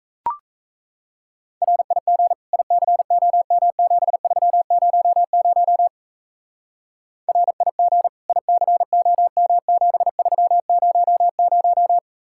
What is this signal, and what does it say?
Morse code, heard as a keyed tone, about 35 wpm, 700 Hz. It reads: RIG ICOM7300 RIG ICOM7300
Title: Speed-Racing Common QSO Phrases Encore - 28wpm